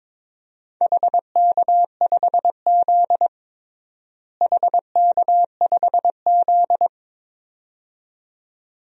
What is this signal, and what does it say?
Morse code, heard as a keyed tone, about 22 wpm, 700 Hz. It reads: HK5Z HK5Z
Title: DXCC-100 Call Signs - 4 Characters - Repeated 2x - 22wpm